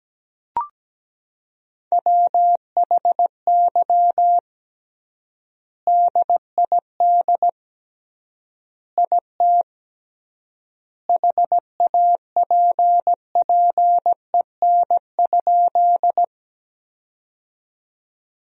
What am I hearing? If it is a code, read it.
Morse code, heard as a keyed tone, about 17 wpm, 700 Hz. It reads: WHY DID IT HAPPEN?
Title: 3x Word-Spacing - Sentences from Top 400 Words - 17wpm